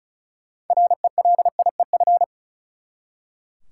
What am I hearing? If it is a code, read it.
Morse code, heard as a keyed tone, about 36 wpm, 700 Hz. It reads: RELIEF
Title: Six Letter Words - 35wpm